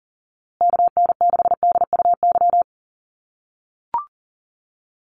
Morse code, heard as a keyed tone, about 40 wpm, 700 Hz. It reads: KN6DUY
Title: USA Call Signs Repeated 3x 40wpm